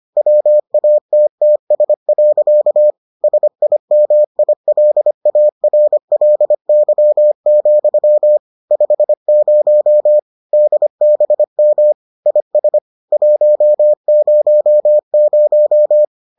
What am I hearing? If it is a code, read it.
Morse code, heard as a keyed tone, about 25 wpm, 600 Hz. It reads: WATTS. SIMILARLY, 50 DBM IS 100